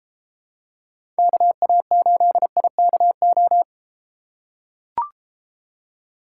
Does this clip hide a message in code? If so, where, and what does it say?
Yes: Morse code from 1.2 to 3.6 s, heard as a keyed tone, about 33 wpm, 700 Hz. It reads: KA8IKO